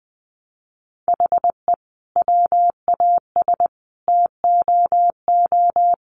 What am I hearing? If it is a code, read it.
Morse code, heard as a keyed tone, about 20 wpm, 700 Hz. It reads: HE WAS TOO
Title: Sentences from Top 1000 Words 20wpm